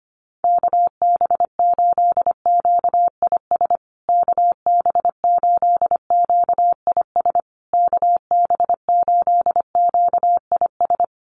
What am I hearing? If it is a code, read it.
Morse code, heard as a keyed tone, about 25 wpm, 700 Hz. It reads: KB8QIS KB8QIS KB8QIS
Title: USA Call Signs Repeated 3x 25wpm